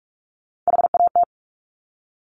Morse code, heard as a keyed tone, about 46 wpm, 700 Hz. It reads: HAT
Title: Top Three Letter Combinations in Words 45wpm